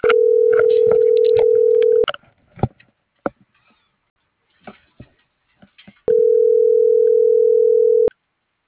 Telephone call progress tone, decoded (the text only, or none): ringback tone